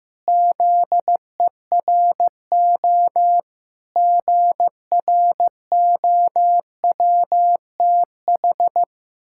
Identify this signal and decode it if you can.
Morse code, heard as a keyed tone, about 16 wpm, 700 Hz. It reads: ZERO GROWTH
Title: Pangram Challenge v2 - 15wpm